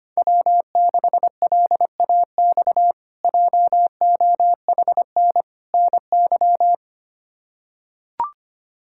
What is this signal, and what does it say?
Morse code, heard as a keyed tone, about 25 wpm, 700 Hz. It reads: W6LAX JOHN NY
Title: K1USN SST QSO Practice - 25wpm